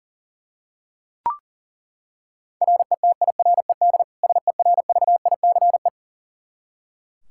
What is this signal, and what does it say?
Morse code, heard as a keyed tone, about 40 wpm, 700 Hz. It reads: RETIRED SERVICE
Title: K1USN SST QSO Practice 40wpm